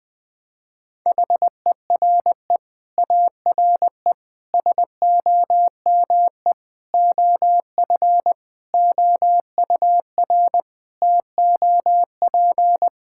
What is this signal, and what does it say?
Morse code, heard as a keyed tone, about 20 wpm, 700 Hz. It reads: HERE ARE SOME OF OUR TOP
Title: Sets of 9 Words - 20wpm